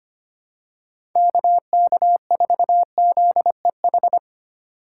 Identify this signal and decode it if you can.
Morse code, heard as a keyed tone, about 25 wpm, 700 Hz. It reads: KK4ZEH
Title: USA Call Signs - 6 Characters - 2x Character Spacing - 25wpm